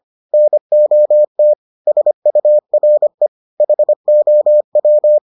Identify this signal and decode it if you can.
Morse code, heard as a keyed tone, about 25 wpm, 600 Hz. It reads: NOT SURE HOW